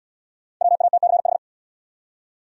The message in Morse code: WATCH

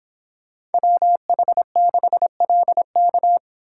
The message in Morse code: WH6LK

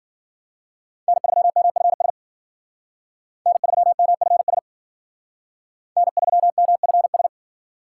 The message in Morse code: N3MFS N3MFS N3MFS

